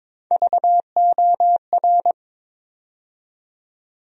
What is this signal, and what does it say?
Morse code, heard as a keyed tone, about 22 wpm, 700 Hz. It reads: VOR
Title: USA Call Sign Postfixes - 3 Letters 22wpm